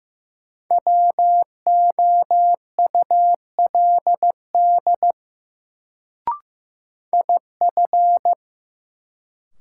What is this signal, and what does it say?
Morse code, heard as a keyed tone, about 15 wpm, 700 Hz. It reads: WOULD IF
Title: Sets of 2 Words - New Word Drill - 15wpm